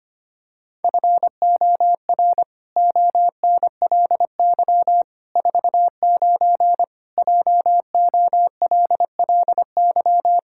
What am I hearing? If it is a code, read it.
Morse code, heard as a keyed tone, about 25 wpm, 700 Hz. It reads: FOR ONLY 49 JOLLY
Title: Pangram Challenge v2 - 25wpm